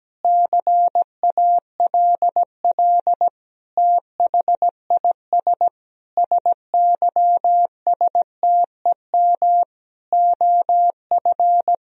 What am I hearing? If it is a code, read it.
Morse code, heard as a keyed tone, about 17 wpm, 700 Hz. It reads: CALL THIS SYSTEM OF
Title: Sentences from Top 1000 Words - Review Part 3 - 17wpm